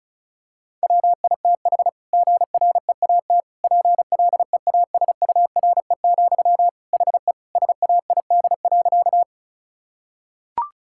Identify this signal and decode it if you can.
Morse code, heard as a keyed tone, about 34 wpm, 700 Hz. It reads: WITH GREAT PLEASURE, HE SAID.
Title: Sentences from Top 1600 Words 35wpm